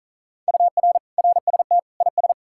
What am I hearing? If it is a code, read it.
Morse code, heard as a keyed tone, about 48 wpm, 700 Hz. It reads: UR RST IS